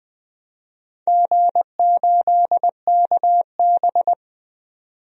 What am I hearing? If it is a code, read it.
Morse code, heard as a keyed tone, about 20 wpm, 700 Hz. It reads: G8KB
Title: DXCC-100 Call Signs - 4 Characters - 20wpm